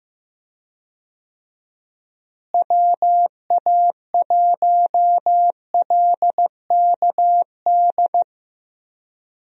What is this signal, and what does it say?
Morse code, heard as a keyed tone, about 15 wpm, 700 Hz. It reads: WA1LKD
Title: USA Call Signs - 6 Characters - Repeated 3x - 15wpm